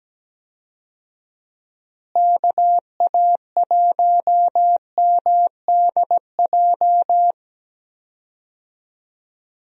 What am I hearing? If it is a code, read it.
Morse code, heard as a keyed tone, about 17 wpm, 700 Hz. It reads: KA1MDJ
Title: USA Call Signs - 6 Characters - Repeated 3x - 17wpm